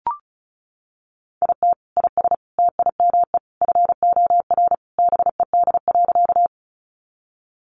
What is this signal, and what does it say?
Morse code, heard as a keyed tone, about 34 wpm, 700 Hz. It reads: IT IS TIME FOR BED.
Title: Sentences from Top 500 Words 35wpm